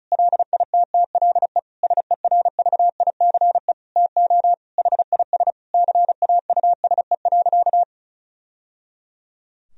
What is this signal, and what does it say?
Morse code, heard as a keyed tone, about 36 wpm, 700 Hz. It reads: LITTLE SERVICE TO HIS CAUSE.
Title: Sentences from Top 1000 Words - Review Part 4 - 35wpm